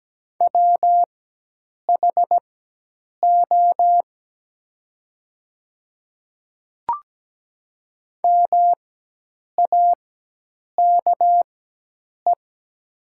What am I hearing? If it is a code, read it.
Morse code, heard as a keyed tone, about 17 wpm, spaced as for 8 wpm, 700 Hz. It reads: WHO MAKE
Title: 4x Character Spacing - Top 100 Words - 17wpm